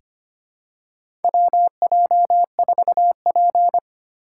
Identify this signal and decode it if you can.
Morse code, heard as a keyed tone, about 25 wpm, 700 Hz. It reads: WJ4P
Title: USA Call Signs - 4 Characters - 8x Character Spacing - 25wpm